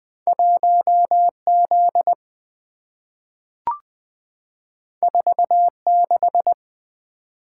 Morse code, heard as a keyed tone, about 20 wpm, 700 Hz. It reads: 1Z 46
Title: Two Letter-Number Combinations - 20wpm